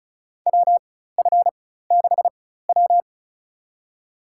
Morse code, heard as a keyed tone, about 35 wpm, spaced as for 16 wpm, 700 Hz. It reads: WF6W
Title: USA Call Signs - 4 Characters - 4x Character Spacing - 35wpm